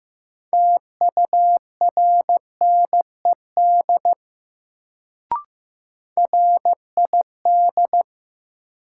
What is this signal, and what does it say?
Morse code, heard as a keyed tone, about 15 wpm, 700 Hz. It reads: TURNED RID
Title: Sets of 4 Words - New Word Drill - 15wpm